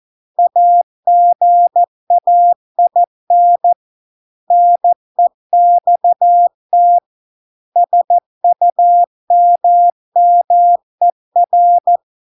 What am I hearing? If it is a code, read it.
Morse code, heard as a keyed tone, about 14 wpm, 700 Hz. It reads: AGAIN NEXT SUMMER